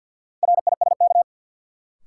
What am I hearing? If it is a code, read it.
Morse code, heard as a keyed tone, about 53 wpm, 700 Hz. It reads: RISK
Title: Sentences from Top 600 Words - New Word Drill - 50wpm